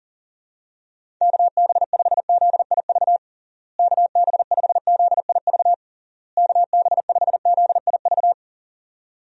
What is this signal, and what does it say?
Morse code, heard as a keyed tone, about 40 wpm, 700 Hz. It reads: KB5ZIV KB5ZIV KB5ZIV
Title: USA Call Signs - 6 Characters - Repeated 3x - 40wpm